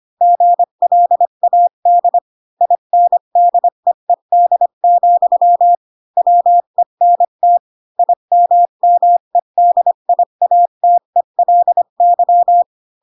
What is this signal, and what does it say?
Morse code, heard as a keyed tone, about 25 wpm, 700 Hz. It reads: GLAD INDEED, WENT IMMEDIATELY